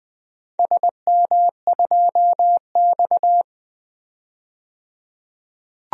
Morse code, heard as a keyed tone, about 20 wpm, 700 Hz. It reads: SM2X
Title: DXCC-100 Call Signs - 4 Characters - 20wpm